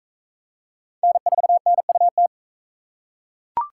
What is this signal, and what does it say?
Morse code, heard as a keyed tone, about 39 wpm, 700 Hz. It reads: N4NUT